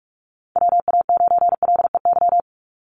Morse code, heard as a keyed tone, about 44 wpm, 700 Hz. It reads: WA9LEY
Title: Call Signs Repeated 2x - 45wpm